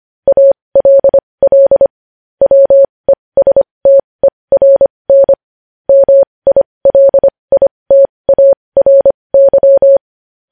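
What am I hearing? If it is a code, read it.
Morse code, heard as a keyed tone, about 25 wpm, spaced as for 20 wpm, 550 Hz. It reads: ALL WESTERN MILITARY
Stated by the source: Morse code news headlines practice at 20.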